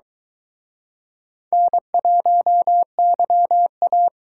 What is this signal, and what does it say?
Morse code, heard as a keyed tone, about 23 wpm, 700 Hz. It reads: N1YA